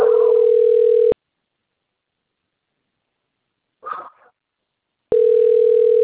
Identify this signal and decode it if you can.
Ringback tone (North America), 440 + 480 Hz.